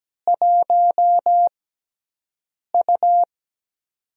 Morse code, heard as a keyed tone, about 17 wpm, spaced as for 6 wpm, 700 Hz. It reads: 1U
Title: USA Call Signs - 6x Character Spacing - 17wpm